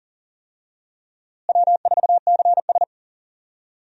Morse code, heard as a keyed tone, about 40 wpm, 700 Hz. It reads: W4CS
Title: USA Call Signs - 4 Characters - Repeated 2x - 40wpm